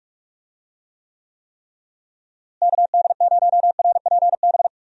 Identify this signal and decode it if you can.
Morse code, heard as a keyed tone, about 41 wpm, 700 Hz. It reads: KD0RPB
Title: USA Call Signs - 6 Characters - Repeated 3x - 45wpm